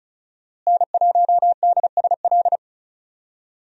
Morse code, heard as a keyed tone, about 34 wpm, 700 Hz. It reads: N1DSL
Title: USA Call Signs - 5 Characters - Repeated 2x - 35wpm